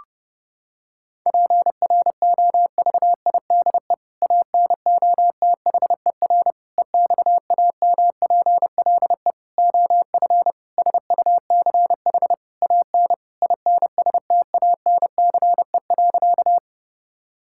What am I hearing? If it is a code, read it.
Morse code, heard as a keyed tone, about 32 wpm, 700 Hz. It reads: PROVIDE ANOTHER EXAMPLE OF SUCH AN INSTANCE.
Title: Sentences from Top 1000 Words - Review Part 5 - 30wpm